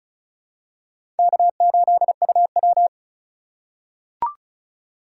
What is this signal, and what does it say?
Morse code, heard as a keyed tone, about 36 wpm, 700 Hz. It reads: K8UW
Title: USA Call Signs - 4 Characters - 3x Character Spacing - 35wpm